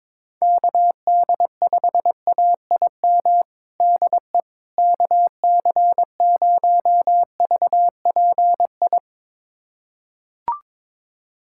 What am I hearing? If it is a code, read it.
Morse code, heard as a keyed tone, about 21 wpm, 700 Hz. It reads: KD5AIM DE KC0VPI
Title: USA Callsign DE Callsign 22wpm